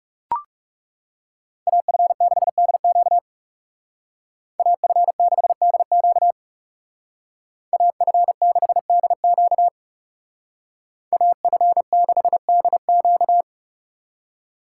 Morse code, heard as a keyed tone, about 36 wpm, 700 Hz. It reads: AF6DQ AF6DQ AF6DQ AF6DQ
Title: Speed Racing - USA 2x2 Call Signs 30wpm